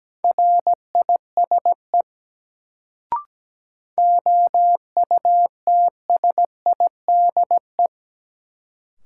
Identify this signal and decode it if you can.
Morse code, heard as a keyed tone, about 17 wpm, 700 Hz. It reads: RISE OUTSIDE
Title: Sentences from Top 900 Words - New Word Drill - 17wpm